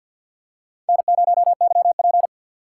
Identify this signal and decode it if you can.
Morse code, heard as a keyed tone, about 46 wpm, 700 Hz. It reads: N0YP
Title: USA Call Signs - 4 Characters - 3x Character Spacing - 50wpm